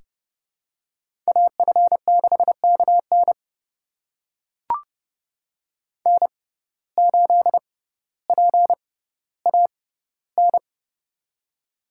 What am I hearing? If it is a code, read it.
Morse code, heard as a keyed tone, about 30 wpm, spaced as for 15 wpm, 700 Hz. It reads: AF6KN N8PAN
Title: USA Call Signs - 5 Characters - 6x Character Spacing - 30wpm